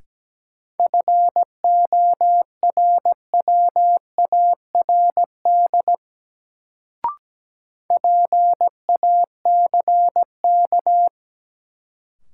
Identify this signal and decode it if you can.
Morse code, heard as a keyed tone, about 17 wpm, 700 Hz. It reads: FORWARD PACK
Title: Sentences from Top 1300 Words - New Word Drill Alt - 17wpm